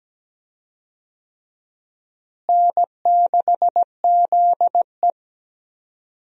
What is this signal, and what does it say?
Morse code, heard as a keyed tone, about 17 wpm, 700 Hz. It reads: N6ZE
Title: USA Call Signs - 4 Characters - Repeated 3x - 17wpm